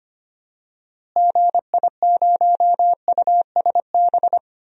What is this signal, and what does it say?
Morse code, heard as a keyed tone, about 25 wpm, 700 Hz. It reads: GI0USB
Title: DXCC-100 Call Signs - 6 Characters - Repeated 3x - 25wpm